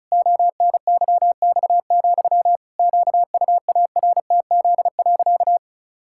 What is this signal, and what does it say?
Morse code, heard as a keyed tone, about 36 wpm, 700 Hz. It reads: ONYX, QUARTZ.
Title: Pangram Challenge v2 - 35wpm